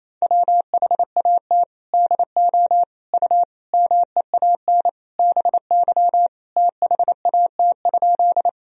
Morse code, heard as a keyed tone, about 27 wpm, 700 Hz. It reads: WHAT DO U MEAN BY THAT?
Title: Sentences from Top 200 Words and Abbreviations 28wpm